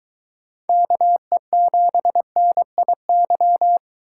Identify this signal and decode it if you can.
Morse code, heard as a keyed tone, about 24 wpm, 700 Hz. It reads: KE7NIY